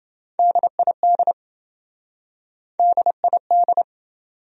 Morse code, heard as a keyed tone, about 27 wpm, 700 Hz. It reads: DID DID